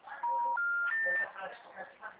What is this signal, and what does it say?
Special information tone, with speech.
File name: out- -2013- - - .48